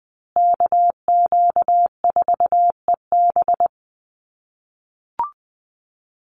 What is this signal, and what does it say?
Morse code, heard as a keyed tone, about 19 wpm, 700 Hz. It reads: KQ4EB
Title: USA Call Signs Repeated 3x 20wpm